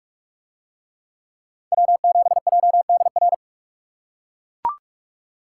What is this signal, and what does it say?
Morse code, heard as a keyed tone, about 43 wpm, 700 Hz. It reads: W7JDR